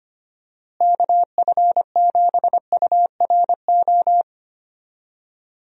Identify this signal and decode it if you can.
Morse code, heard as a keyed tone, about 25 wpm, 700 Hz. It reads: KF7URO
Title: USA Call Signs - 6 Characters - Repeated 2x - 25wpm